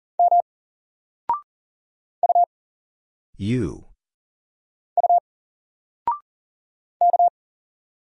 Morse code, heard as a keyed tone, about 40 wpm, 700 Hz. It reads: M U U K
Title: Single Letter-Number 40wpm